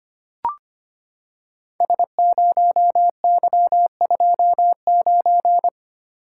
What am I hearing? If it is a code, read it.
Morse code, heard as a keyed tone, about 25 wpm, 700 Hz. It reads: S0Y29
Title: Code Groups - 5 Letter:Numbers - 25wpm